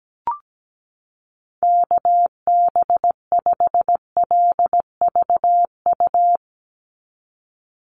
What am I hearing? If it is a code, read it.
Morse code, heard as a keyed tone, about 17 wpm, 700 Hz. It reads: KB5LVU
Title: USA Call Signs 17wpm